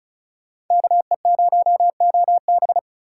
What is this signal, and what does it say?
Morse code, heard as a keyed tone, about 46 wpm, 700 Hz. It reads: KE0OB